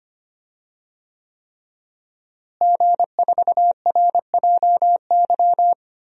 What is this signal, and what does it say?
Morse code, heard as a keyed tone, about 25 wpm, 700 Hz. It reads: G4RJY
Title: DXCC-100 Call Signs - 5 Characters - Repeated 3x - 25wpm